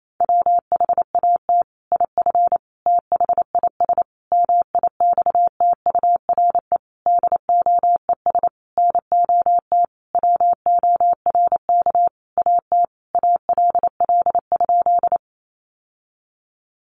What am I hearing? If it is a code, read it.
Morse code, heard as a keyed tone, about 28 wpm, 700 Hz. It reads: WHAT IF THIS MIXTURE DOES NOT WORK AT ALL?
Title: Sentences from Top 1500 Words 28wpm